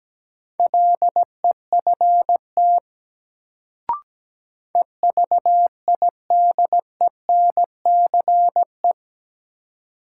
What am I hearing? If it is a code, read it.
Morse code, heard as a keyed tone, about 17 wpm, 700 Hz. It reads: LEFT EVIDENCE
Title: Sets of 5 Words Encore - New Word Drill - 17wpm